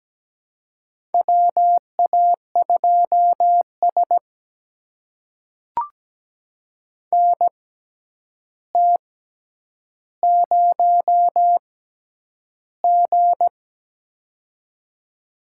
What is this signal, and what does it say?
Morse code, heard as a keyed tone, about 17 wpm, spaced as for 9 wpm, 700 Hz. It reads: WA2S NT0G
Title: USA Call Signs - 6x Character Spacing - 17wpm